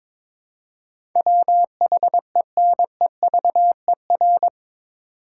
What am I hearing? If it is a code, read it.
Morse code, heard as a keyed tone, about 22 wpm, 700 Hz. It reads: WHENEVER
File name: practice / 22wpm / Eight Letter Words - 22wpm